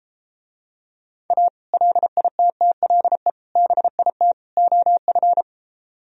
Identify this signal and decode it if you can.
Morse code, heard as a keyed tone, about 33 wpm, 700 Hz. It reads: A LITTLE BIT OF